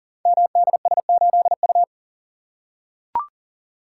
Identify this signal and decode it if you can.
Morse code, heard as a keyed tone, about 38 wpm, 700 Hz. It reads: MDS8U
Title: Code Groups - 5 Letter:Numbers - 40wpm